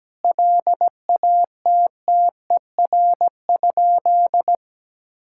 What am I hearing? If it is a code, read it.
Morse code, heard as a keyed tone, about 17 wpm, 700 Hz. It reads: LATTER?
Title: Sentences from Top 1600 Words 17wpm